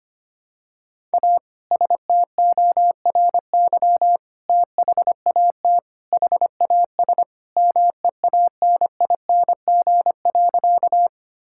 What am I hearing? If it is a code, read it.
Morse code, heard as a keyed tone, about 25 wpm, 700 Hz. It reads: A STORY THAT HAS MEANING.
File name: practice / 25wpm / Sentences with 2-Word Phrases - 25wpm